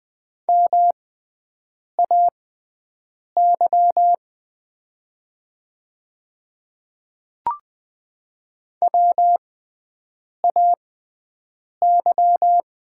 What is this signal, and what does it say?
Morse code, heard as a keyed tone, about 20 wpm, spaced as for 7 wpm, 700 Hz. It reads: MAY WAY
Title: 6x Character Spacing - Top 300 Words - 20wpm